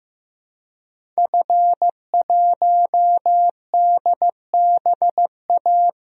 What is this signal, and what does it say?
Morse code, heard as a keyed tone, about 15 wpm, 700 Hz. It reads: F1DBA
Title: DXCC-100 Call Signs - 5 Characters - Repeated 3x - 15wpm